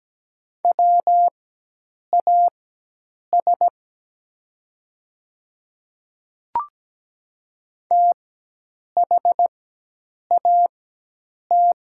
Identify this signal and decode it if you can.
Morse code, heard as a keyed tone, about 17 wpm, spaced as for 8 wpm, 700 Hz. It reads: WAS THAT
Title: 4x Character Spacing - Top 100 Words - 17wpm